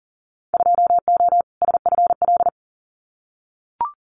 Morse code, heard as a keyed tone, about 40 wpm, 700 Hz. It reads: 2O SFL
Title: Winter Field Day Exchanges 40wpm